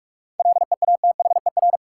Morse code, heard as a keyed tone, about 46 wpm, 700 Hz. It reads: LEATHER
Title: Seven Letter Words - 45wpm